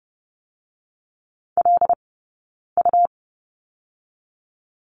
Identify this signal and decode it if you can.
Morse code, heard as a keyed tone, about 30 wpm, 700 Hz. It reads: L U
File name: Lesson 17d - Letter u - Two Characters 30wpm